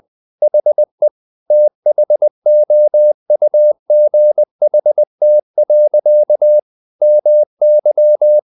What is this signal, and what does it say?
Morse code, heard as a keyed tone, about 20 wpm, 600 Hz. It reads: HE THOUGHT. MY
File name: the old man and the sea part 2 20wpm